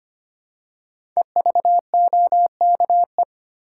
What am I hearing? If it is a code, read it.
Morse code, heard as a keyed tone, about 25 wpm, 700 Hz. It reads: EVOKE
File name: 5 Letter Words - First 3 Letters Given - 25wpm